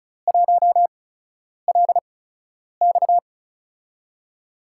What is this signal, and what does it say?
Morse code, heard as a keyed tone, about 36 wpm, spaced as for 10 wpm, 700 Hz. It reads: 1LX
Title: USA Call Signs - 5 Characters - 8x Character Spacing - 35wpm